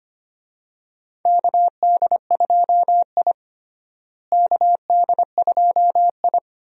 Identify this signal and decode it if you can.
Morse code, heard as a keyed tone, about 25 wpm, 700 Hz. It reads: KD2I KD2I
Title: USA Call Signs - 4 Characters - Repeated 2x - 25wpm